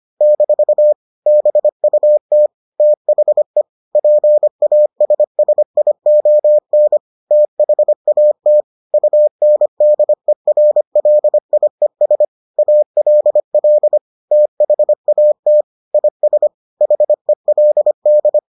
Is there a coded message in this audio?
Yes: Morse code from 0.2 to 18.4 s, heard as a keyed tone, about 25 wpm, 600 Hz. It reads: - BUT THE PASSION THAT UNDERLIES ALL THAT IS HELD